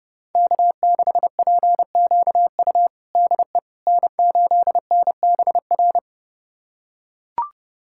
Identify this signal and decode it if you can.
Morse code, heard as a keyed tone, about 31 wpm, 700 Hz. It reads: K6PQU DE N8NBR
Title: USA Callsign DE Callsign - 5 Characters - 30wpm